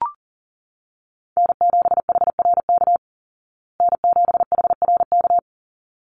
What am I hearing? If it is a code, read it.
Morse code, heard as a keyed tone, about 40 wpm, 700 Hz. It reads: N7HRK N7HRK
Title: Call Signs Repeated 3x - 40wpm